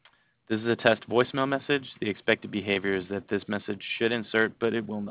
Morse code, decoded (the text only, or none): none